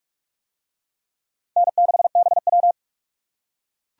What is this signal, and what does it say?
Morse code, heard as a keyed tone, about 44 wpm, 700 Hz. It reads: N6BW